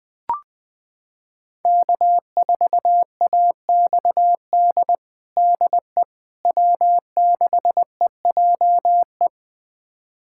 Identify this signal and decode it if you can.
Morse code, heard as a keyed tone, about 20 wpm, 700 Hz. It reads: K4AXD DE W6EJE
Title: USA Callsign DE Callsign 20wpm